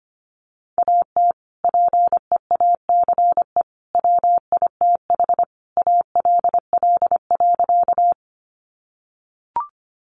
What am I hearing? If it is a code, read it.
Morse code, heard as a keyed tone, about 25 wpm, 700 Hz. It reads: AT PEACE WITH ALL.
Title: Sentences from Top 1500 Words 25wpm